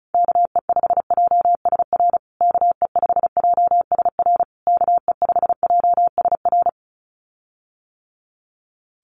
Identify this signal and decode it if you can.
Morse code, heard as a keyed tone, about 34 wpm, 700 Hz. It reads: KE5JSR KE5JSR KE5JSR
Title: USA Call Signs Repeated 3x 35wpm